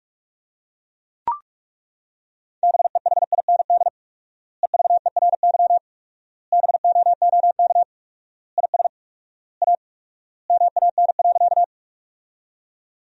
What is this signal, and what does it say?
Morse code, heard as a keyed tone, about 42 wpm, 700 Hz. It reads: BEHIND EVERY BOOK IS A MAN.
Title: 4x Word-Spacing - Sentences from Top 500 Words - 45wpm